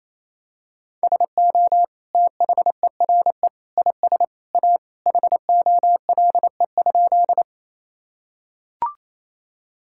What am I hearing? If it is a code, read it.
Morse code, heard as a keyed tone, about 28 wpm, 700 Hz. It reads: SO THERE IS A HOLE?
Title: Sentences from Top 1600 Words 28wpm